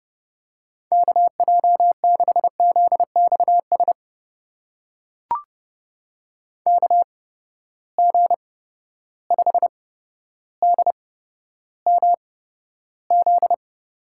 Morse code, heard as a keyed tone, about 30 wpm, spaced as for 13 wpm, 700 Hz. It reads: KJ6ZXS KG5DMZ
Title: USA Call Signs - 6 Characters - 8x Character Spacing - 30wpm